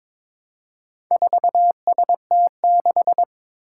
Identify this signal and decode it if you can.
Morse code, heard as a keyed tone, about 22 wpm, 700 Hz. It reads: 4ST6
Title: Four Challenging Letter-Number Combinations 22wpm